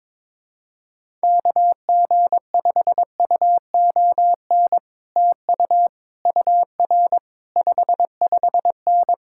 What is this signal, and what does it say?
Morse code, heard as a keyed tone, about 22 wpm, 700 Hz. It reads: KG5UON TU UR 55N
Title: Speed-Racing Common QSO Phrases - 22wpm